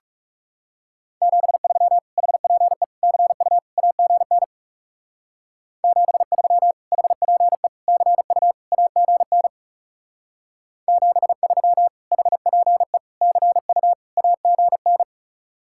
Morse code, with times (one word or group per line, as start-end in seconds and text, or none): 1.2-2.0 73
2.2-2.8 HPE
3.0-3.6 CU
3.8-4.4 AGN
5.8-6.7 73
6.9-7.7 HPE
7.9-8.5 CU
8.7-9.5 AGN
10.9-11.9 73
12.1-13.0 HPE
13.2-13.9 CU
14.2-15.0 AGN